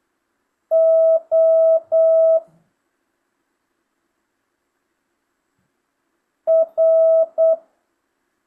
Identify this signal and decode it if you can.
Morse code, heard as a keyed tone, about 8 wpm, 650 Hz. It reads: O R
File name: jh8cbh / 2025cw